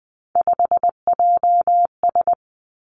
Morse code, heard as a keyed tone, about 20 wpm, 700 Hz. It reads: 5JS
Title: Three Challenging Letter-Number Combinations 20wpm